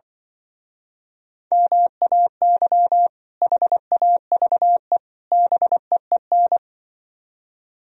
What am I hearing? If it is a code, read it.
Morse code, heard as a keyed tone, about 24 wpm, 700 Hz. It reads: MAY HAVE BEEN
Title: Speed-Racing - Sets of 3 Words 20wpm